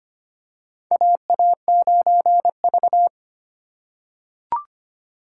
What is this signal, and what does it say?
Morse code, heard as a keyed tone, about 25 wpm, 700 Hz. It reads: AA9V
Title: USA Call Signs - 4 Characters - 3x Character Spacing - 25wpm